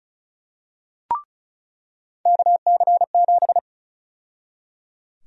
Morse code, heard as a keyed tone, about 36 wpm, 700 Hz. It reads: KC7
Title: USA Call Sign Prefixes - 35wpm